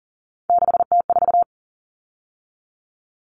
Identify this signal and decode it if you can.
Morse code, heard as a keyed tone, about 40 wpm, 700 Hz. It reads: 6T4
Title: Common Cut Numbers - 3-Digit 40wpm